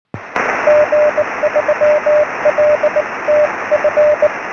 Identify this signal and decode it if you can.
Morse code, heard as a keyed tone, about 19 wpm, 600 Hz, with radio noise. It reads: G3LTF
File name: sm7lcb / g3ltf 23cm